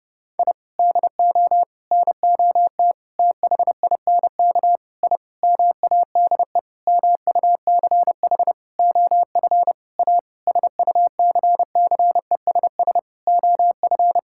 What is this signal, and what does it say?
Morse code, heard as a keyed tone, about 32 wpm, 700 Hz. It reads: I DO NOT THINK I MADE MUCH OF A SUCCESS OF
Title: Sentences from Top 1000 Words - Review Part 6 - 30wpm